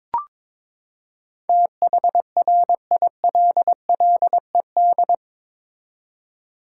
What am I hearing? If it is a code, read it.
Morse code, heard as a keyed tone, about 21 wpm, 700 Hz. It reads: THRILLED
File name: practice / 22wpm / Eight Letter Words - 22wpm